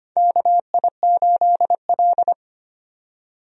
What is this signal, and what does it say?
Morse code, heard as a keyed tone, about 25 wpm, 700 Hz. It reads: KI8L
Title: USA Call Signs - 4 Characters - 25wpm